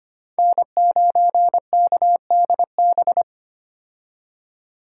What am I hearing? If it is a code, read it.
Morse code, heard as a keyed tone, about 25 wpm, 700 Hz. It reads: N9KDB
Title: K1USN SST QSO Practice - 25wpm